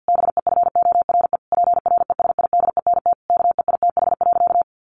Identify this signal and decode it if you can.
Morse code, heard as a keyed tone, about 50 wpm, 700 Hz. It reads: BEFORE PRESIDENT KEITH.
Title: A Sound Of Thunder 50wpm